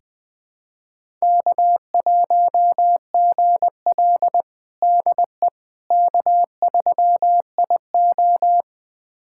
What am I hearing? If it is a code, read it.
Morse code, heard as a keyed tone, about 20 wpm, 700 Hz. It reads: K1GL DE K3IO
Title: USA Callsign DE Callsign - 4 Characters - 20wpm